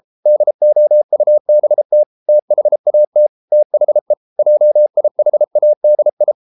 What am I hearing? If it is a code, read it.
Morse code, heard as a keyed tone, about 33 wpm, 600 Hz. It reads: DOUBT THAT THE JIHADI